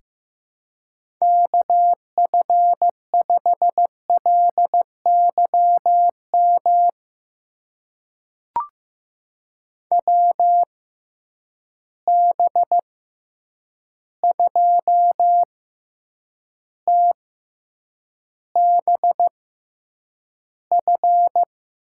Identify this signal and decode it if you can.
Morse code, heard as a keyed tone, about 15 wpm, spaced as for 8 wpm, 700 Hz. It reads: KF5LYM WB2TBF
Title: USA Call Signs - 6 Characters - 6x Character Spacing - 15wpm